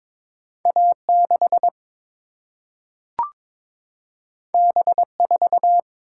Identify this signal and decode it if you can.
Morse code, heard as a keyed tone, about 22 wpm, 700 Hz. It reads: A 6 B 4
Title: Single-Letter Single-Number 22wpm